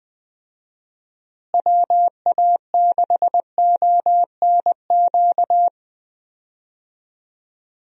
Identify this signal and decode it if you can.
Morse code, heard as a keyed tone, about 20 wpm, 700 Hz. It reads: WA6ONQ
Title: USA Call Signs - 6 Characters - Repeated 2x - 20wpm